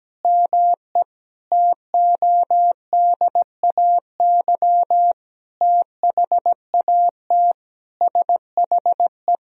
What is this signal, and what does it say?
Morse code, heard as a keyed tone, about 17 wpm, 700 Hz. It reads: ME TODAY THAT SHE
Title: Sentences from Top 1000 Words - Review Part 4 - 17wpm